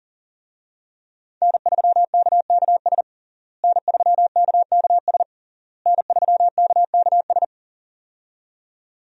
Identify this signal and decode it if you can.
Morse code, heard as a keyed tone, about 40 wpm, 690 Hz. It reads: N3KKS N3KKS N3KKS
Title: USA Call Signs - 5 Characters - Repeated 3x - 40wpm